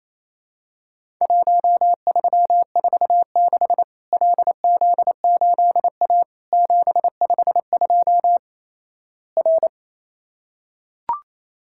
Morse code, heard as a keyed tone, about 28 wpm, 700 Hz. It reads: 1346 LZ8A 752 R
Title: Worked All Europe DX Contest - QTC Exchange - 28wpm